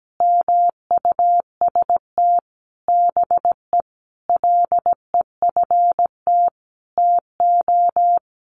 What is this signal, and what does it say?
Morse code, heard as a keyed tone, about 17 wpm, 700 Hz. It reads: MUST BE LEFT TO
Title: Sentences from Top 1100 Words 17wpm